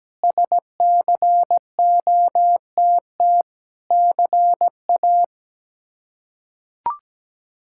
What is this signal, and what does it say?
Morse code, heard as a keyed tone, about 17 wpm, 700 Hz. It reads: SCOTT CA
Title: K1USN SST QSO Practice - 17wpm